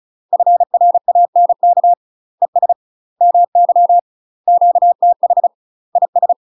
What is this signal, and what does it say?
Morse code, heard as a keyed tone, about 34 wpm, 700 Hz. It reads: FRANK ES MY QTH IS
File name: UK MASTER QSOs TRAINING / 35WPM / UK MASTER QSOs 0003